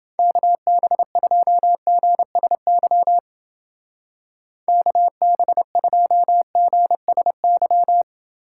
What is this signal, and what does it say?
Morse code, heard as a keyed tone, about 30 wpm, 700 Hz. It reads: KB2GSY KB2GSY